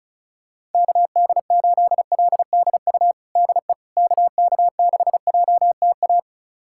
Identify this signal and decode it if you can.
Morse code, heard as a keyed tone, about 36 wpm, 700 Hz. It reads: KD8LDU DE KK6JTA
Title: USA Callsign DE Callsign - 6 Characters - 35wpm